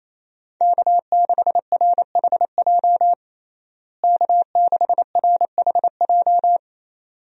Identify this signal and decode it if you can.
Morse code, heard as a keyed tone, about 28 wpm, 700 Hz. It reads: K6RHJ K6RHJ
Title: USA Call Signs - 5 Characters - Repeated 2x - 28wpm